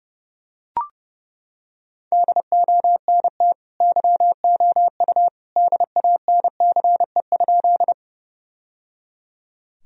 Morse code, heard as a keyed tone, about 30 wpm, 700 Hz. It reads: DONT YOU DANCE?
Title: Sentences from Top 1000 Words - Review Part 4 - 30wpm